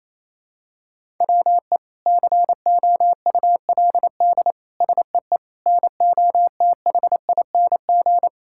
Morse code, heard as a keyed tone, about 28 wpm, 700 Hz. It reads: WE COULD SEE NOTHING
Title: Sentences from Top 1000 Words - Review Part 5 - 28wpm